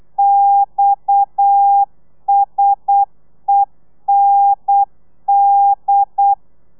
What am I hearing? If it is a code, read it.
Morse code, heard as a keyed tone, about 8 wpm, 800 Hz. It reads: XSEND